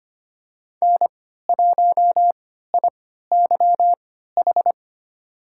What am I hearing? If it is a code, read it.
Morse code, heard as a keyed tone, about 25 wpm, spaced as for 14 wpm, 700 Hz. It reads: N1IYH
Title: USA Call Signs - 3x Character Spacing - 25wpm